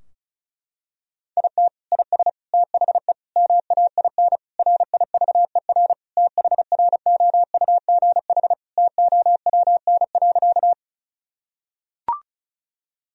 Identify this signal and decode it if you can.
Morse code, heard as a keyed tone, about 34 wpm, 700 Hz. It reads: IT IS THE MAIN RIVER THROUGH TOWN.
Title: Sentences from Top 1000 Words - Review Part 2 - 35wpm